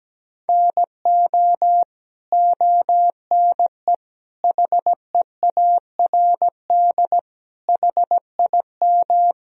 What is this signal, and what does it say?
Morse code, heard as a keyed tone, about 17 wpm, 700 Hz. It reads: NO ONE HEARD HIM
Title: Sentences from Top 1000 Words - Review Part 1 - 17wpm